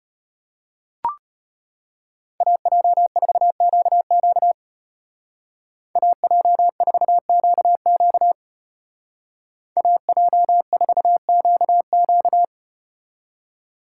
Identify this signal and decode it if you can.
Morse code, heard as a keyed tone, about 34 wpm, 700 Hz. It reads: AJ4QQ AJ4QQ AJ4QQ